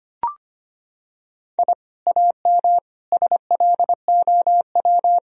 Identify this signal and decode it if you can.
Morse code, heard as a keyed tone, about 25 wpm, 700 Hz. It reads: I AM SLOW
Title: Sentences from Top 1000 Words - Review Part 2 - 25wpm